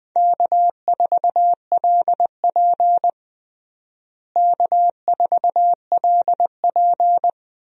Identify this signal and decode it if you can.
Morse code, heard as a keyed tone, about 21 wpm, 700 Hz. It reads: K4LP K4LP
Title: USA Call Signs - 4 Characters - Repeated 2x - 20wpm